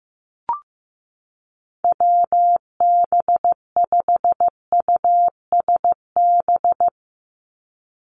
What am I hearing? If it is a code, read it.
Morse code, heard as a keyed tone, about 15 wpm, 700 Hz. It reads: WB5USB
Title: NA SOTA Activator Callsigns 15wpm